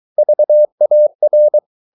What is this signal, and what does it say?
Morse code, heard as a keyed tone, about 23 wpm, 600 Hz. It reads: VAR
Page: Ferie 23 wpm